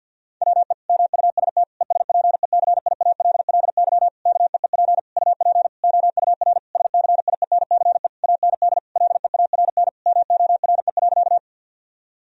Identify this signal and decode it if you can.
Morse code, heard as a keyed tone, about 50 wpm, 700 Hz. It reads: WE MUST ESPECIALLY KEEP UP OUR SCIENCE AND LEARN MORE.
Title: Sentences from Top 1000 Words - Review Part 4 - 50wpm